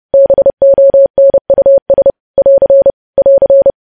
Morse code, heard as a keyed tone, about 30 wpm, 550 Hz. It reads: BONUS <AR> <AR>
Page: news headlines in morse code at 30 wpm